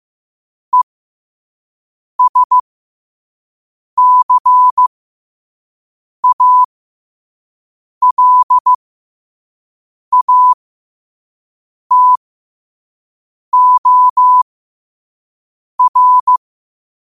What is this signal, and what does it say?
Morse code, heard as a keyed tone, about 15 wpm, spaced as for 5 wpm, 1000 Hz. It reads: ESCALATOR
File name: qotdfiles / 5wpm